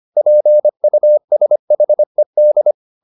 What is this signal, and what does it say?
Morse code, heard as a keyed tone, about 24 wpm, 600 Hz. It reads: PUSHED